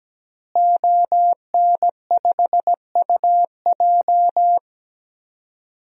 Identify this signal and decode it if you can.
Morse code, heard as a keyed tone, about 17 wpm, 700 Hz. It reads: ON5UJ
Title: DXCC-100 Call Signs - 5 Characters - Repeated 2x - 17wpm